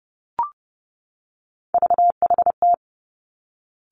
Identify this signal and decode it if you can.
Morse code, heard as a keyed tone, about 30 wpm, 700 Hz. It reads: VHT
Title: Three Challenging Letter-Number Combinations 30wpm